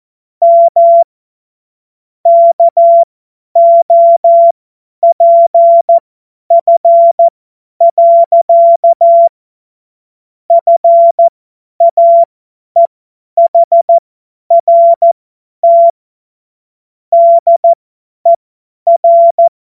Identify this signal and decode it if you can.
Morse code, heard as a keyed tone, about 14 wpm, spaced as for 10 wpm, 680 Hz. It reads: M KOPF. FAEHRT DER
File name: Morsekurs / Witze / Witz 19 März 2024 50 F70